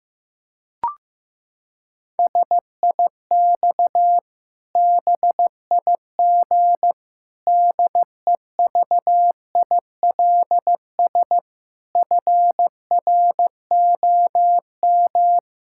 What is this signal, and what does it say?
Morse code, heard as a keyed tone, about 16 wpm, 700 Hz. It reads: SIX BIG DEVILS FROM
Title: Pangram Challenge v2 - 15wpm